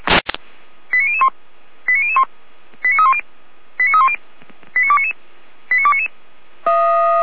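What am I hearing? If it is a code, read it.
Five-tone selective call (ZVEI), heard as quick runs of five tones, tones of about 70 ms. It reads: 89001 89001 89310 89310 89299 89299
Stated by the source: Five-tone ZVEI selective call received by SDR, audio.f85360.feuerwehr-orlamuende.org.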